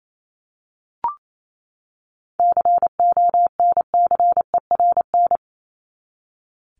Morse code, heard as a keyed tone, about 27 wpm, 700 Hz. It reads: CONCERN
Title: Top 751-1000 Words 28wpm